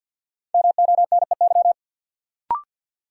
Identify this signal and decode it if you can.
Morse code, heard as a keyed tone, about 47 wpm, 700 Hz. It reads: MONEY
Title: Sets of 4 Words - New Word Drill - 50wpm